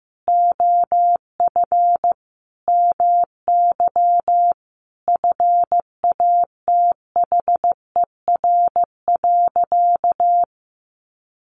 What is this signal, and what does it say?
Morse code, heard as a keyed tone, about 15 wpm, 700 Hz. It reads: OF MY FATHER.
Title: Sentences from Top 300 Words 15wpm